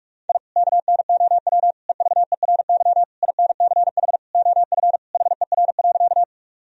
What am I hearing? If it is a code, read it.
Morse code, heard as a keyed tone, about 46 wpm, 700 Hz. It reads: I KNOW EVERY INCH OF HER.
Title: Sentences from Top 1000 Words - Review Part 3 - 45wpm